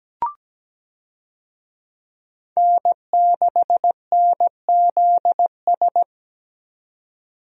Morse code, heard as a keyed tone, about 17 wpm, 700 Hz. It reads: N6NZS
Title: USA Call Signs - 5 Characters - 17wpm